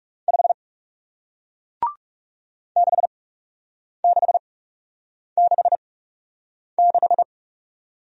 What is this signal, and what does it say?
Morse code, heard as a keyed tone, about 40 wpm, spaced as for 8 wpm, 700 Hz. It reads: 5 6666